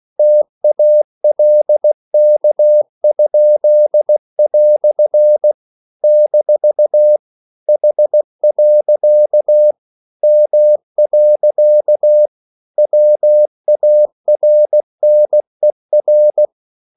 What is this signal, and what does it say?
Morse code, heard as a keyed tone, about 16 wpm, 600 Hz. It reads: TALK?" - H. M. WARNER
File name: CW / Practice / EnglishQuotations0416